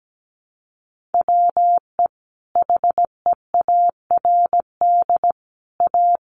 Morse code, heard as a keyed tone, about 17 wpm, 700 Hz. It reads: WE HEARD A
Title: Sentences from Top 1500 Words 17wpm